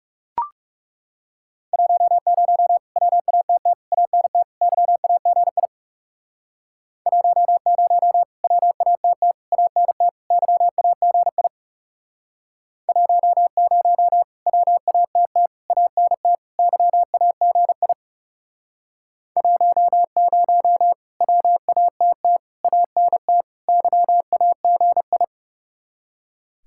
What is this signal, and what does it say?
Morse code, heard as a keyed tone, about 40 wpm, 700 Hz. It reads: 10 WATT ANT YAGI 10 WATT ANT YAGI 10 WATT ANT YAGI 10 WATT ANT YAGI